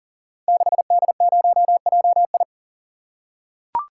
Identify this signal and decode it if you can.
Morse code, heard as a keyed tone, about 40 wpm, 700 Hz. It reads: 6D0JI
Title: Code Groups - 5 Letter:Numbers - 40wpm